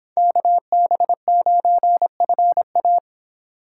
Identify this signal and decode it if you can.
Morse code, heard as a keyed tone, about 25 wpm, 700 Hz. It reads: KB9FA